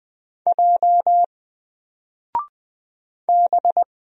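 Morse code, heard as a keyed tone, about 20 wpm, 700 Hz. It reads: J B
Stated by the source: Single letter and number Morse drill at 20 wpm, from N4LCD.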